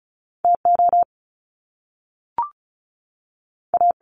Morse code, heard as a keyed tone, about 36 wpm, spaced as for 6 wpm, 700 Hz. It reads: TOA
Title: Top Two Letter Combinations in Words 35wpm